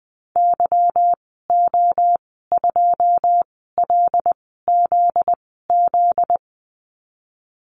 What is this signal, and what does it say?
Morse code, heard as a keyed tone, about 19 wpm, spaced as for 14 wpm, 700 Hz. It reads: YO2LZZ
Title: 245 Lesson 39g - Letter z - Call Signs 20wpm